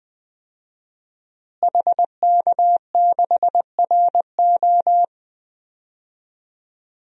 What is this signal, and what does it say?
Morse code, heard as a keyed tone, about 21 wpm, 700 Hz. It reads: HK6RO
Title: DXCC-100 Call Signs - 5 Characters - Repeated 3x - 20wpm